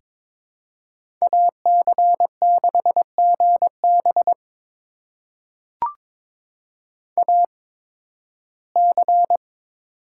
Morse code, heard as a keyed tone, about 22 wpm, spaced as for 14 wpm, 700 Hz. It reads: AC6GB AC
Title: USA Call Signs - 5 Characters - 8x Character Spacing - 22wpm